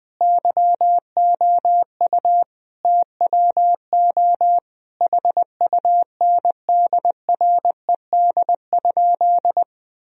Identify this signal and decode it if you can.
Morse code, heard as a keyed tone, about 20 wpm, 700 Hz. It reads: YOU TWO HUNDRED?
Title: Sentences from Top 1000 Words - Review Part 2 - 20wpm